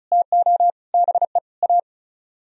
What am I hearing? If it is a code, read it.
Morse code, heard as a keyed tone, about 42 wpm, 700 Hz. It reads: TO BE A
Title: Speed-Racing - Sets of 5 Words 35wpm